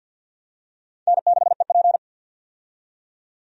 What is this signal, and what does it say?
Morse code, heard as a keyed tone, about 53 wpm, 700 Hz. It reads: N6EP
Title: USA Call Signs - 4 Characters - 50wpm